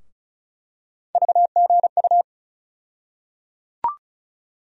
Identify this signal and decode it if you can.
Morse code, heard as a keyed tone, about 34 wpm, 700 Hz. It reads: VGU
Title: USA Call Sign Postfixes - 3 Letters 35wpm